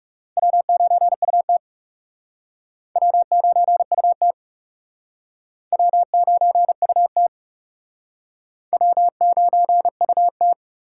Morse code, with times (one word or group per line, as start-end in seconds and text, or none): 0.4-1.6 W9UT
3.0-4.3 W9UT
5.7-7.3 W9UT
8.7-10.5 W9UT